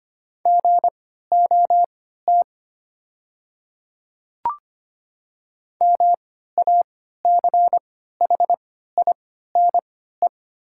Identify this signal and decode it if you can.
Morse code, heard as a keyed tone, about 25 wpm, spaced as for 14 wpm, 700 Hz. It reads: GOT MACHINE
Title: 3x Character Spacing - Top 500 Words - 25wpm